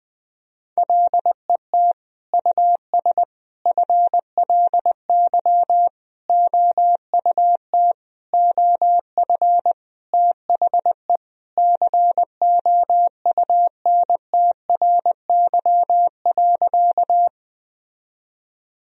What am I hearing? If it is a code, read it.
Morse code, heard as a keyed tone, about 20 wpm, 700 Hz. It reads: LET US FLY OUT OF THE COUNTRY.
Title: Sentences from Top 1000 Words - Review Part 2 - 20wpm